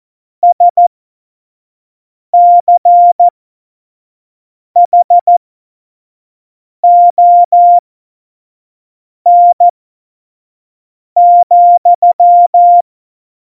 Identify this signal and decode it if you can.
Morse code, heard as a keyed tone, about 14 wpm, spaced as for 5 wpm, 700 Hz. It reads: SCHON,
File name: Morsekurs / Witze / Witz 01. September 2020, 25 F70